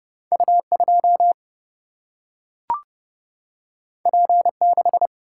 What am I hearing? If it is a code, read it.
Morse code, heard as a keyed tone, about 30 wpm, 700 Hz. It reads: U2 P6
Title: Single-Letter Single-Number 30wpm